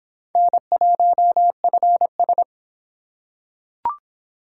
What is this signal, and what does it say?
Morse code, heard as a keyed tone, about 26 wpm, 700 Hz. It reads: N1FS